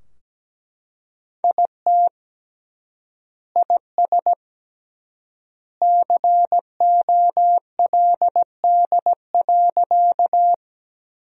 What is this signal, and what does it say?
Morse code, heard as a keyed tone, about 17 wpm, 700 Hz. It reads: IT IS COLD.
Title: 3x Word-Spacing - Sentences from Top 400 Words - 17wpm